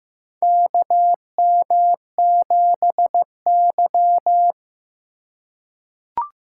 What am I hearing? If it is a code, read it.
Morse code, heard as a keyed tone, about 15 wpm, 700 Hz. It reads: KM7Y
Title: USA Call Signs - 4 Characters - 4x Character Spacing - 15wpm